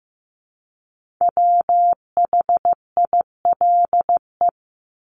Morse code, heard as a keyed tone, about 15 wpm, 700 Hz. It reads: WHILE
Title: Top 300 Words Repeated 15wpm